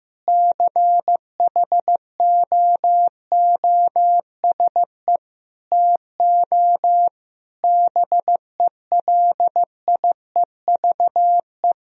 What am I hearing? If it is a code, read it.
Morse code, heard as a keyed tone, about 15 wpm, 700 Hz. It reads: CHOOSE TO BELIEVE
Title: Sentences from Top 1000 Words - Review Part 4 - 15wpm